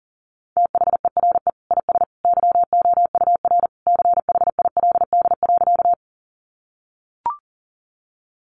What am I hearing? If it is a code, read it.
Morse code, heard as a keyed tone, about 40 wpm, 690 Hz. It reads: THERE IS YOUR CHILD.
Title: Sentences from Top 800 Words 40wpm